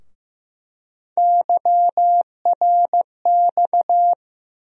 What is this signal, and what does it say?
Morse code, heard as a keyed tone, about 15 wpm, 700 Hz. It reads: YRX
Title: USA Call Sign Postfixes - 3 Letters 15wpm